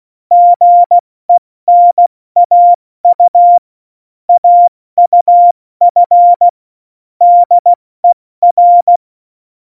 Morse code, heard as a keyed tone, about 16 wpm, spaced as for 14 wpm, 700 Hz. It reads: GENAU AUF DER
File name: Morsekurs / Witze / Witz 08. Dezember 2020, 70 F80